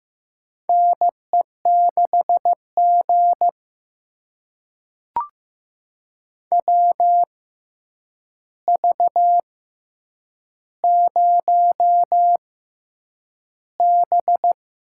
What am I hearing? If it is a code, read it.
Morse code, heard as a keyed tone, about 15 wpm, spaced as for 8 wpm, 700 Hz. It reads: NE6G WV0B
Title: USA Call Signs - 4 Characters - 6x Character Spacing - 15wpm